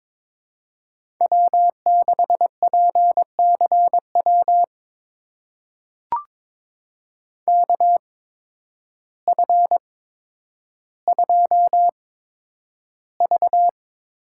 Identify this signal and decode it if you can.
Morse code, heard as a keyed tone, about 22 wpm, spaced as for 10 wpm, 700 Hz. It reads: W6PCW KF2V
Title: USA Call Signs - 5 Characters - 8x Character Spacing - 22wpm